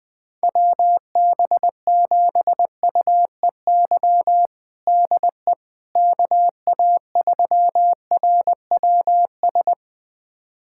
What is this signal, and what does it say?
Morse code, heard as a keyed tone, about 20 wpm, 700 Hz. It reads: WB7UEY DE KA3RWS
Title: USA Callsign DE Callsign - 6 Characters - 20wpm